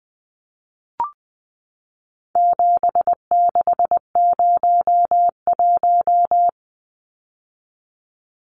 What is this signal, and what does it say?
Morse code, heard as a keyed tone, about 20 wpm, 700 Hz. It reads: 7601
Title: Numbers - 4-Digit 20wpm